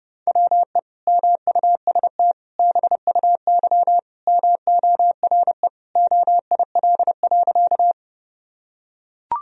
Morse code, heard as a keyed tone, about 28 wpm, 700 Hz. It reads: WE MUST BUY MORE OIL.
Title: Sentences from Top 1000 Words - Review Part 4 - 30wpm